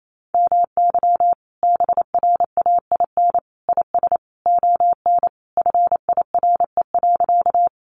Morse code, heard as a keyed tone, about 28 wpm, 700 Hz. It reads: MY BRAIN IS ON FIRE.
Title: Sentences from Top 1100 Words 28wpm